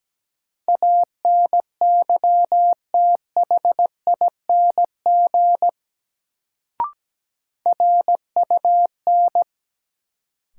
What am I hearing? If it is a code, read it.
Morse code, heard as a keyed tone, about 17 wpm, 700 Hz. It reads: ANYTHING RUN
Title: Sets of 4 Words - New Word Drill - 17wpm